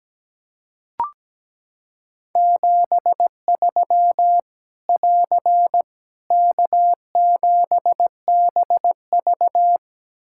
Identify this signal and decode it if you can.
Morse code, heard as a keyed tone, about 17 wpm, 700 Hz. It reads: 73 <AR> K7BV
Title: K1USN SST QSO Practice 17wpm